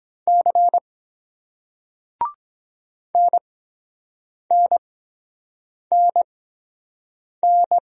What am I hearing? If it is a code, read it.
Morse code, heard as a keyed tone, about 24 wpm, spaced as for 7 wpm, 700 Hz. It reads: C NNNN